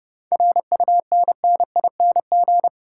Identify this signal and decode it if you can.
Morse code, heard as a keyed tone, about 37 wpm, 700 Hz. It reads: RUNNING
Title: Speed-Racing Common QSO Phrases - 30wpm